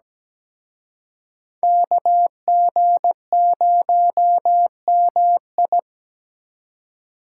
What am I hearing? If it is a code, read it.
Morse code, heard as a keyed tone, about 20 wpm, 700 Hz. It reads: KG0MI
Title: Speed Racing - USA Call Signs 17wpm